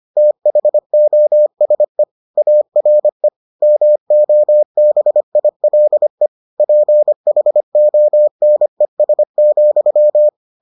Morse code, heard as a keyed tone, about 25 wpm, 600 Hz. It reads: THOSE ARE MOBILE PHONES,